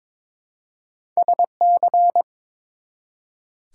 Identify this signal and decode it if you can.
Morse code, heard as a keyed tone, about 22 wpm, 700 Hz. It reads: SC
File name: ARRL Sections v2 - 22wpm